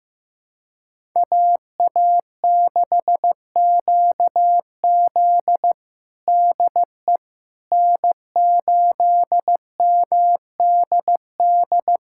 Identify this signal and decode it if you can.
Morse code, heard as a keyed tone, about 16 wpm, 700 Hz. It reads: AA6QZ DE N8MDD
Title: USA Callsign DE Callsign - 5 Characters - 15wpm